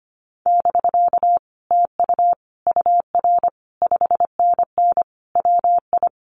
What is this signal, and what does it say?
Morse code, heard as a keyed tone, about 25 wpm, 700 Hz. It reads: <BK> TU UR 5NN WI
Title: Common QSO Phrases Encore 25wpm